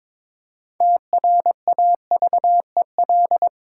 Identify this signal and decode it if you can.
Morse code, heard as a keyed tone, about 22 wpm, 700 Hz. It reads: TRAVEL
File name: practice / 22wpm / Sentences with 3-Word Phrases - 22wpm